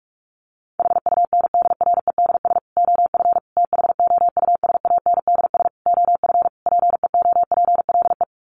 Morse code, heard as a keyed tone, about 44 wpm, 700 Hz. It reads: HUNDREDS OF THOUSANDS OF PEOPLE
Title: Sets of 5 Words Encore2 45wpm